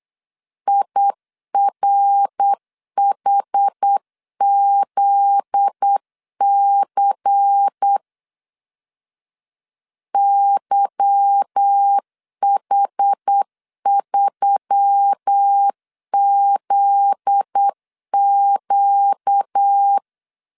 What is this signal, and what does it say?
Morse code, heard as a keyed tone, about 8 wpm, 800 Hz. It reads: IRHZC YH3ZQ
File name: LEZIONI / 8 YJ45 / 8 YJ45 8wpm 3214